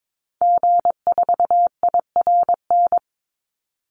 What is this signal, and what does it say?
Morse code, heard as a keyed tone, about 22 wpm, 700 Hz. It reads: G4IRN
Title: Call Signs - 22wpm